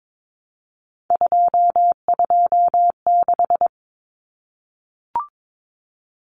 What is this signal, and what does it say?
Morse code, heard as a keyed tone, about 22 wpm, 700 Hz. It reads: 226
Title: Common Cut Numbers - 3-Digit 22wpm